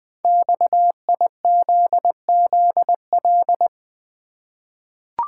Morse code, heard as a keyed tone, about 20 wpm, 700 Hz. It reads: XIZZL
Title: Code Groups - 5 Letters - 20wpm